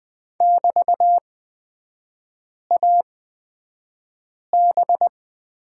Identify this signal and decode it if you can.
Morse code, heard as a keyed tone, about 20 wpm, spaced as for 5 wpm, 700 Hz. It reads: <BT>AB
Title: 224 Lesson 36e - BT - Three Characters 20wpm